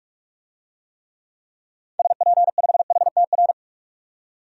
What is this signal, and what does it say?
Morse code, heard as a keyed tone, about 42 wpm, 700 Hz. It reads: SP5HTR
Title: DXCC-100 Call Signs - 6 Characters - 45wpm